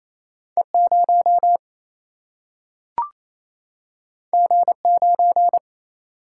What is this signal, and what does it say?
Morse code, heard as a keyed tone, about 28 wpm, 700 Hz. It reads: E 0 G 9
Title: Single-Letter Single-Number 28wpm